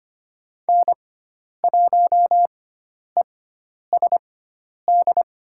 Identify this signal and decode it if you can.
Morse code, heard as a keyed tone, about 25 wpm, spaced as for 10 wpm, 700 Hz. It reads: N1ESD
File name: Lesson 16g - Letter c - Call Signs 25wpm